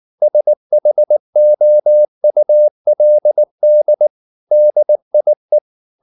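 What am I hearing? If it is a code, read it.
Morse code, heard as a keyed tone, about 19 wpm, 600 Hz. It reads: SHOULD DIE
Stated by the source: Morse code at 19 wpm from CW med Gnister, reading Ken Follett's novel Never in English.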